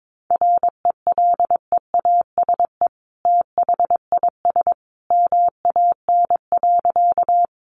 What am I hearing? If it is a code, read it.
Morse code, heard as a keyed tone, about 22 wpm, 700 Hz. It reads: RELEASE THIS MAN.
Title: Sentences from Top 1100 Words 22wpm